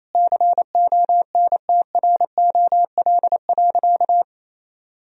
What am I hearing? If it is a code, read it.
Morse code, heard as a keyed tone, about 28 wpm, 700 Hz. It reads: CONTROL.
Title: Sentences from Top 1000 Words - Review Part 4 - 28wpm